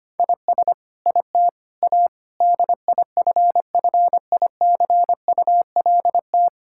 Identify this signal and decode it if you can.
Morse code, heard as a keyed tone, about 25 wpm, 700 Hz. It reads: IS IT A DIFFICULT
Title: Sentences from Top 1000 Words - Review Part 6 - 25wpm